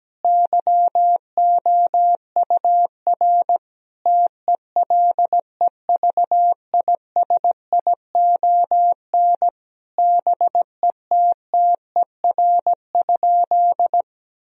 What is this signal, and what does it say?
Morse code, heard as a keyed tone, about 17 wpm, 700 Hz. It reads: YOUR TELEVISION BETTER?
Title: Sentences from Top 1000 Words - Review Part 4 - 17wpm